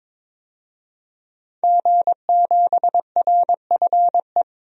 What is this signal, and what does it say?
Morse code, heard as a keyed tone, about 22 wpm, 700 Hz. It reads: G7RFE
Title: DXCC-100 Call Signs - 5 Characters - 22wpm